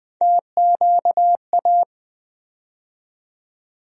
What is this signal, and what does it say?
Morse code, heard as a keyed tone, about 21 wpm, 700 Hz. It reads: TQA
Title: USA Call Sign Postfixes - 3 Letters 20wpm